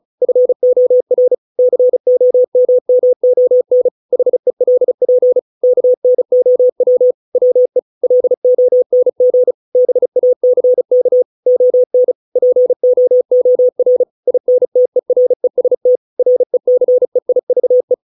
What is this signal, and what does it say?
Morse code, heard as a keyed tone, about 35 wpm, 500 Hz. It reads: FOR COMMON HELP KNOW WE LONG BACK ON POOR INTEREST RECEIVE